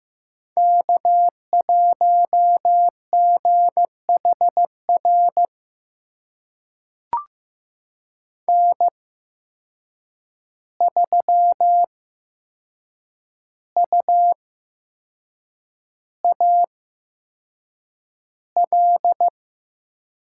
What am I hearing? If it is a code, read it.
Morse code, heard as a keyed tone, about 15 wpm, spaced as for 6 wpm, 700 Hz. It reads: K1GHR N3UAL
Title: USA Call Signs - 5 Characters - 8x Character Spacing - 15wpm